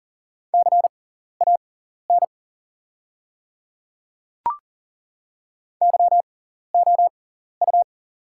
Morse code, heard as a keyed tone, about 40 wpm, spaced as for 14 wpm, 700 Hz. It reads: CAN YOU
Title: 6x Character Spacing - Top 100 Words - 40wpm